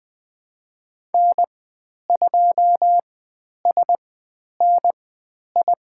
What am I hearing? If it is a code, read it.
Morse code, heard as a keyed tone, about 20 wpm, spaced as for 10 wpm, 700 Hz. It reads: N2SNI
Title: 095 Lesson 16g - Letter c - Call Signs 20wpm